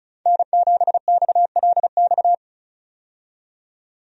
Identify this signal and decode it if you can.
Morse code, heard as a keyed tone, about 36 wpm, 700 Hz. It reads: N7XLX
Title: USA Call Signs - 5 Characters - 35wpm